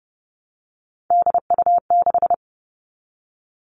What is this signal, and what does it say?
Morse code, heard as a keyed tone, about 30 wpm, 700 Hz. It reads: DU6
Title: All Cut Numbers - 3-Digit 30wpm